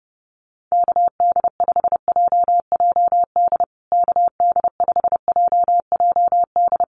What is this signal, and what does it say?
Morse code, heard as a keyed tone, about 30 wpm, 700 Hz. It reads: KD5JJD KD5JJD
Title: USA Call Signs Repeated 3x 30wpm